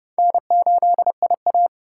Morse code, heard as a keyed tone, about 29 wpm, 700 Hz. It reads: N8IA